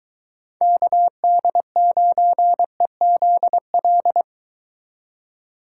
Morse code, heard as a keyed tone, about 22 wpm, 700 Hz. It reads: KD9EZL